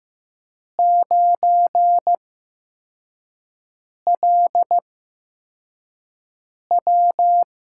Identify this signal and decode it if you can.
Morse code, heard as a keyed tone, about 15 wpm, spaced as for 4 wpm, 700 Hz. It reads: 9LW
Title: USA Call Signs - 5 Characters - 8x Character Spacing - 15wpm